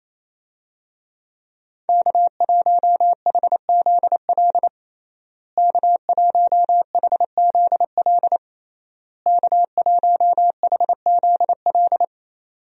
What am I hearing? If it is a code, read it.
Morse code, heard as a keyed tone, about 28 wpm, 700 Hz. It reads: K1HZL K1HZL K1HZL
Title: USA Call Signs - 5 Characters - Repeated 3x - 28wpm